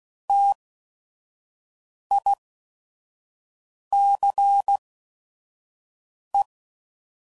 Morse code, heard as a keyed tone, about 16 wpm, spaced as for 5 wpm, 800 Hz. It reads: TICE
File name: CW Code Groups @5WPM / QSOs / WD0FBT 1